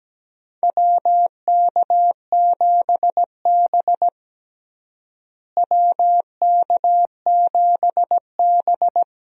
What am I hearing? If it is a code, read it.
Morse code, heard as a keyed tone, about 17 wpm, 700 Hz. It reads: WK7B WK7B
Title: USA Call Signs - 4 Characters - Repeated 3x - 17wpm